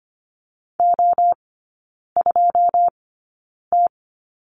Morse code, heard as a keyed tone, about 25 wpm, spaced as for 9 wpm, 690 Hz. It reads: O2T
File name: Lesson 14e - Number 2 - Three Characters 25wpm